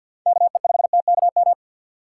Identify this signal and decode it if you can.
Morse code, heard as a keyed tone, about 53 wpm, 700 Hz. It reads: KE5TKM
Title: USA Call Signs - 6 Characters - 6x Character Spacing - 50wpm